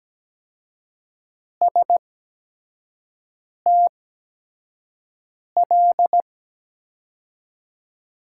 Morse code, heard as a keyed tone, about 17 wpm, spaced as for 5 wpm, 700 Hz. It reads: STL
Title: USA Call Signs - 5 Characters - 8x Character Spacing - 17wpm